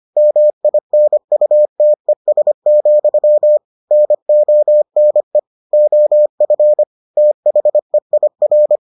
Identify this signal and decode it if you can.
Morse code, heard as a keyed tone, about 25 wpm, 600 Hz. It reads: MINUTES, NONE OF THEIR